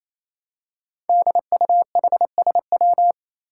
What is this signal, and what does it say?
Morse code, heard as a keyed tone, about 27 wpm, 700 Hz. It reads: DUHSW
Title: Five Challenging Letter-Number Combinations 28wpm